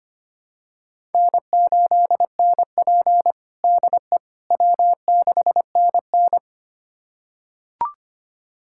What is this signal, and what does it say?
Morse code, heard as a keyed tone, about 25 wpm, 700 Hz. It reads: N8NP DE W6NN
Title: USA Callsign DE Callsign - 4 Characters - 25wpm